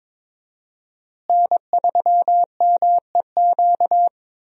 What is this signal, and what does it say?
Morse code, heard as a keyed tone, about 21 wpm, 700 Hz. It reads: N3MEQ
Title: USA Call Signs - 5 Characters - Repeated 2x - 22wpm